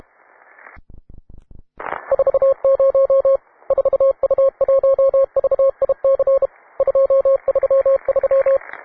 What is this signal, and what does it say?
Morse code, heard as a keyed tone, about 31 wpm, 540 Hz. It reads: 40 4U1VIC 233